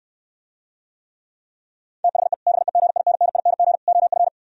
Morse code, heard as a keyed tone, about 66 wpm, 700 Hz. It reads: THE BEGINNING OF